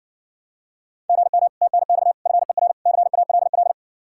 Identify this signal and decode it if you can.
Morse code, heard as a keyed tone, about 69 wpm, 700 Hz. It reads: GM TNX FER CALL